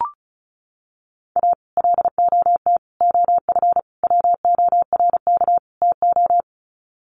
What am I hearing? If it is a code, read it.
Morse code, heard as a keyed tone, about 34 wpm, 700 Hz. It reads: A LOT OF WORK TO
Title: Sets of 5 Words Encore2 35wpm